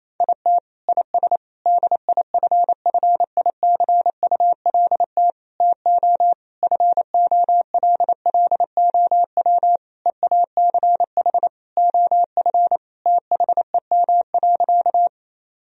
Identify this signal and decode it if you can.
Morse code, heard as a keyed tone, about 28 wpm, 700 Hz. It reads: IT IS DIFFICULT TO FOLLOW EACH OF THEM.
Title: Sentences from Top 1000 Words - Review Part 1 - 28wpm